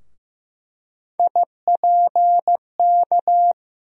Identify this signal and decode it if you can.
Morse code, heard as a keyed tone, about 15 wpm, 700 Hz. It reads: IPK
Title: USA Call Sign Postfixes - 3 Letters 15wpm